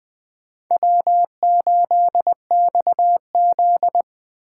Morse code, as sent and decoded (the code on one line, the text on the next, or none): .-- ---.. -..- --..
W8XZ